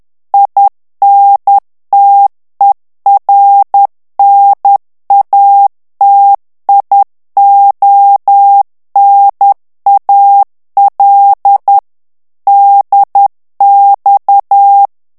Morse code, heard as a keyed tone, about 10 wpm, 800 Hz. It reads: INTERNATIONAL DX
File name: ITALIEN texte clair 2 vitesse 10 mots minute